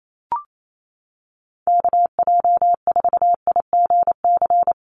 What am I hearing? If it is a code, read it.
Morse code, heard as a keyed tone, about 28 wpm, 700 Hz. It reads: KJ4IGC
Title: USA Call Signs 28wpm